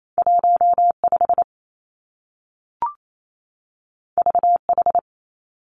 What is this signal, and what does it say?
Morse code, heard as a keyed tone, about 28 wpm, 700 Hz. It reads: 1 5 V H
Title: Two Challenging Letter-Number Combinations 28wpm